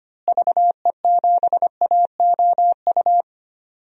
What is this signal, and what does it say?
Morse code, heard as a keyed tone, about 25 wpm, 700 Hz. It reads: VE7AOU
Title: DXCC-100 Call Signs - 6 Characters - Repeated 2x - 25wpm